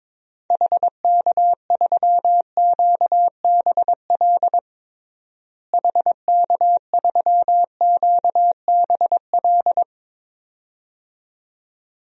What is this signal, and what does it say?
Morse code, heard as a keyed tone, about 22 wpm, 700 Hz. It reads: HK3QBL HK3QBL
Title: DXCC-100 Call Signs - 6 Characters - Repeated 2x - 22wpm